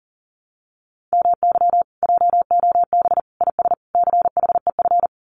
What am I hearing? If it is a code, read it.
Morse code, heard as a keyed tone, about 40 wpm, 700 Hz. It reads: MY JOB IS CHEF
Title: Common QSO Phrases Encore 40wpm